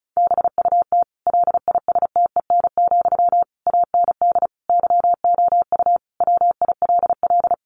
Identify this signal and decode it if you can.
Morse code, heard as a keyed tone, about 34 wpm, 700 Hz. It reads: BUT LISTEN, AND YOU WILL
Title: Sentences from Top 600 Words 35wpm